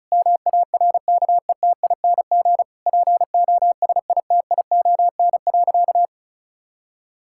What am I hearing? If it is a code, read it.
Morse code, heard as a keyed tone, about 34 wpm, 700 Hz. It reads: MARKETING POSITION.
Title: Sentences from Top 1000 Words - Review Part 5 - 35wpm